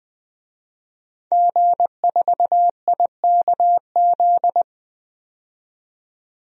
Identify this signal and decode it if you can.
Morse code, heard as a keyed tone, about 20 wpm, 700 Hz. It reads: G4IKZ
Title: DXCC-100 Call Signs - 5 Characters - 20wpm